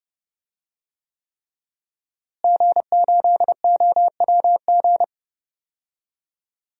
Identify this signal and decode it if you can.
Morse code, heard as a keyed tone, about 30 wpm, 700 Hz. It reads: G8OWG
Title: DXCC-100 Call Signs - 5 Characters - Repeated 3x - 30wpm